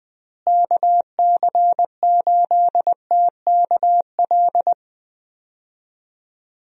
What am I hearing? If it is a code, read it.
Morse code, heard as a keyed tone, about 20 wpm, 700 Hz. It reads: KC8TKL
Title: USA Call Signs - 6 Characters - Repeated 3x - 20wpm